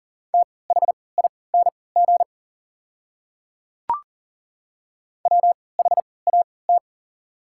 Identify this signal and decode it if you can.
Morse code, heard as a keyed tone, about 39 wpm, spaced as for 22 wpm, 690 Hz. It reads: THING WHAT
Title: 3x Character Spacing - Top 200 Words - 40wpm